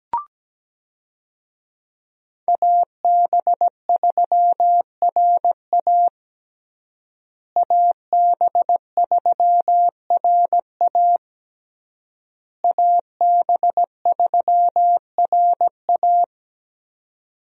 Morse code, heard as a keyed tone, about 17 wpm, 700 Hz. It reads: AB3RA AB3RA AB3RA
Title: USA Call Signs - 5 Characters - Repeated 3x - 17wpm